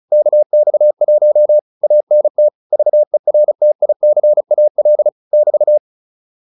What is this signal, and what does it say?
Morse code, heard as a keyed tone, about 35 wpm, 600 Hz. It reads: KX1 ANT VERTICAL <BT>
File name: QSO207 35